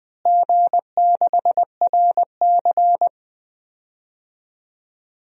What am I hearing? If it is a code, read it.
Morse code, heard as a keyed tone, about 20 wpm, 700 Hz. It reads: G6RC
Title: DXCC-100 Call Signs - 4 Characters - Repeated 2x - 20wpm